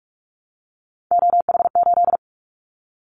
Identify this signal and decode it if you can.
Morse code, heard as a keyed tone, about 44 wpm, 700 Hz. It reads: OH8
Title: Callsign Prefixes for Top 25 DXCC - 45wpm